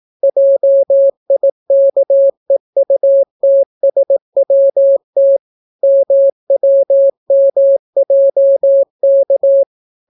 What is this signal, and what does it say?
Morse code, heard as a keyed tone, about 18 wpm, 550 Hz. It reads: JIKEUTSWT MWMJK